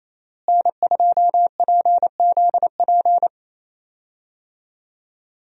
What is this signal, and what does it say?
Morse code, heard as a keyed tone, about 27 wpm, 700 Hz. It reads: N2PZP
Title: USA Call Signs - 5 Characters - Repeated 2x - 28wpm